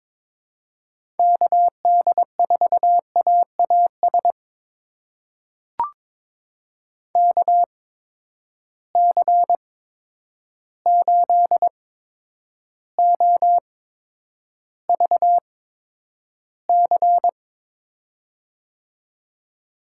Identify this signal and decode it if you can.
Morse code, heard as a keyed tone, about 22 wpm, spaced as for 9 wpm, 700 Hz. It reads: KD4AAS KC8OVC
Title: USA Call Signs - 6 Characters - 8x Character Spacing - 22wpm